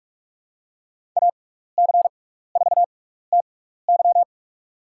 Morse code, heard as a keyed tone, about 46 wpm, spaced as for 15 wpm, 690 Hz. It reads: AC4TY
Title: USA Call Signs - 6x Character Spacing - 45wpm